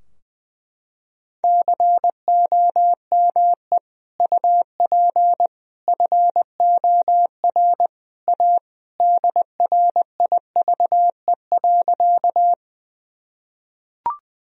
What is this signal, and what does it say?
Morse code, heard as a keyed tone, about 20 wpm, 700 Hz. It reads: COME UP FOR A DRIVE.
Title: Sentences from Top 1000 Words - Review Part 2 - 20wpm